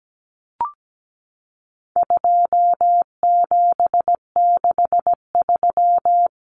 Morse code, heard as a keyed tone, about 17 wpm, 700 Hz. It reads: 2763
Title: Common Cut Numbers - 4-Digit 17wpm